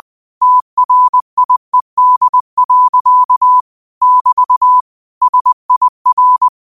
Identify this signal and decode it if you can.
Morse code, heard as a keyed tone, about 20 wpm, 1000 Hz. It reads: TRIED. <BT> SIR